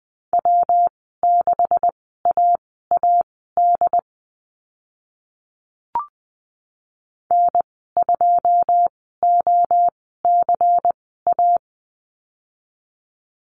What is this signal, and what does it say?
Morse code, heard as a keyed tone, about 20 wpm, spaced as for 14 wpm, 700 Hz. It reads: W6AAD N2OCA
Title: 128 Lesson 21g - Number 6 - Call Signs 20wpm